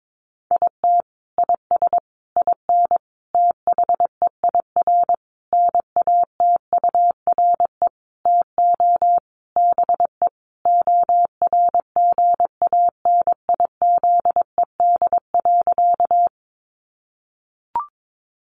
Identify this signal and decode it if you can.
Morse code, heard as a keyed tone, about 22 wpm, 700 Hz. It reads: IT IS IN THEIR NATURE TO BE ORGANIZED.
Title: Sentences from Top 1300 Words 22wpm